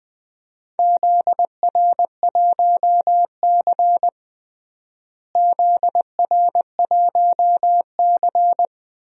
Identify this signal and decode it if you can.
Morse code, heard as a keyed tone, about 20 wpm, 700 Hz. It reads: ZR1C ZR1C
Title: DXCC-100 Call Signs - 4 Characters - Repeated 3x - 20wpm